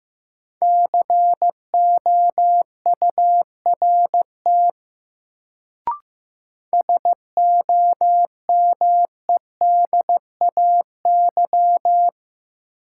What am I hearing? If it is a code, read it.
Morse code, heard as a keyed tone, about 15 wpm, 700 Hz. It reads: COURT SOMEDAY
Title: Sentences from Top 1600 Words - New Word Drill Alt - 15wpm